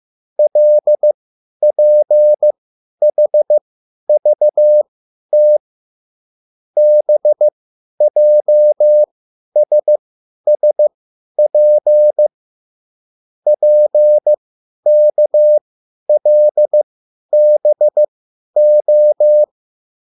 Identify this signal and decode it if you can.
Morse code, heard as a keyed tone, about 15 wpm, spaced as for 10 wpm, 600 Hz. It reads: LPHVT BJSSP PKLBO